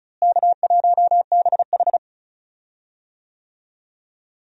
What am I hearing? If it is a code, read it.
Morse code, heard as a keyed tone, about 34 wpm, 700 Hz. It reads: K1BH
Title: USA Call Signs - 4 Characters - Repeated 3x - 35wpm